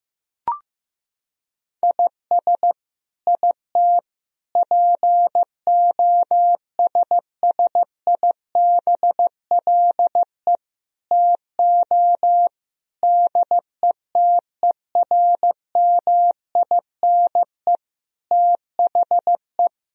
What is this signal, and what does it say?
Morse code, heard as a keyed tone, about 15 wpm, 700 Hz. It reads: IS IT POSSIBLE TO DETERMINE THE
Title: Sentences from Top 1000 Words - Review Part 4 - 15wpm